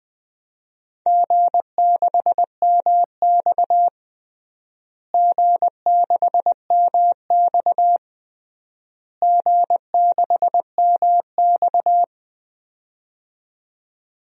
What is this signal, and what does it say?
Morse code, heard as a keyed tone, about 21 wpm, 700 Hz. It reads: G6MX G6MX G6MX
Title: DXCC-100 Call Signs - 4 Characters - Repeated 3x - 20wpm